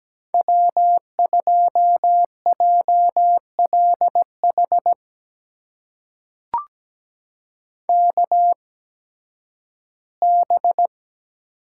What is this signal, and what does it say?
Morse code, heard as a keyed tone, about 17 wpm, spaced as for 11 wpm, 700 Hz. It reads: W2JLH KB
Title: USA Call Signs - 8x Character Spacing - 17wpm